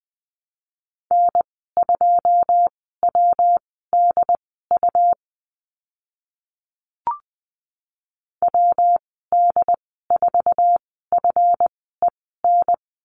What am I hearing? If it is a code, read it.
Morse code, heard as a keyed tone, about 20 wpm, spaced as for 15 wpm, 700 Hz. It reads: N2WDU WD4FEN
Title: 179 Lesson 29g - Slash - Call Signs 20wpm